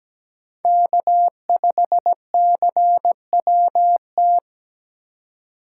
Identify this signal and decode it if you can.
Morse code, heard as a keyed tone, about 20 wpm, 700 Hz. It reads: K5CWT
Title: Speed Racing - USA Call Signs 17wpm